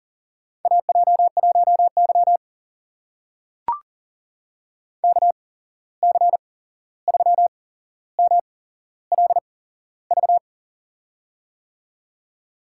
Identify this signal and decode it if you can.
Morse code, heard as a keyed tone, about 38 wpm, spaced as for 15 wpm, 700 Hz. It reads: AJ1Y KC3MLV
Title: USA Call Signs - 8x Character Spacing - 40wpm